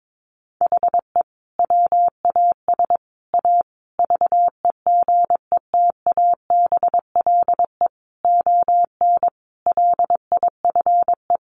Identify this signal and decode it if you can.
Morse code, heard as a keyed tone, about 22 wpm, 700 Hz. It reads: HE WAS A VEGETABLE ON LIFE
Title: Sentences from Top 1300 Words 22wpm